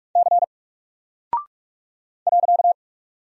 Morse code, heard as a keyed tone, about 45 wpm, 700 Hz. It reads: C .
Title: Single Character (Letter:Number:Punctuation:Prosign) Alt 45wpm